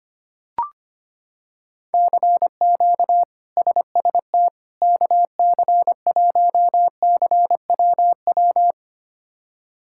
Morse code, heard as a keyed tone, about 25 wpm, 700 Hz. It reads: CQ SST KC1CWW
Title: K1USN SST QSO Practice - 25wpm